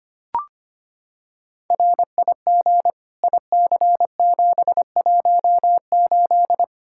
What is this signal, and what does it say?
Morse code, heard as a keyed tone, about 25 wpm, 700 Hz. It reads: RIG IC718
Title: K1USN SST QSO Practice 25wpm